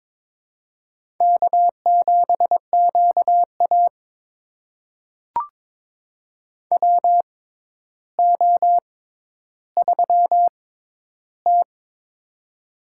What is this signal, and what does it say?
Morse code, heard as a keyed tone, about 22 wpm, spaced as for 11 wpm, 700 Hz. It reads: K7QA WO3T
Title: USA Call Signs - 4 Characters - 6x Character Spacing - 22wpm